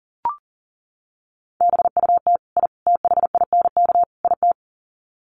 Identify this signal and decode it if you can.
Morse code, heard as a keyed tone, about 40 wpm, 700 Hz. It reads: BUT I THINK IT
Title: Sets of 4 Words 40wpm